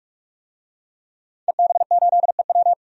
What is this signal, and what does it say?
Morse code, heard as a keyed tone, about 46 wpm, 700 Hz. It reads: EB8EW
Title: DXCC-100 Call Signs - 5 Characters - 45wpm